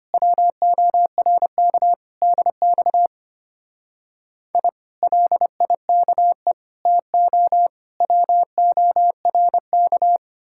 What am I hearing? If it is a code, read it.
Morse code, heard as a keyed tone, about 31 wpm, 700 Hz. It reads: WORK DX I LIKE TO WORK
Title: Speed-Racing Common QSO Phrases - 25wpm